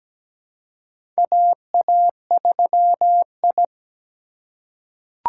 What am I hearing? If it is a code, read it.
Morse code, heard as a keyed tone, about 17 wpm, 700 Hz. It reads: AA3I
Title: USA Call Signs - 4 Characters - 3x Character Spacing - 17wpm